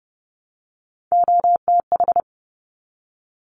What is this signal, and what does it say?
Morse code, heard as a keyed tone, about 30 wpm, 700 Hz. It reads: OTH
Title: Top Three Letter Combinations in Words 30wpm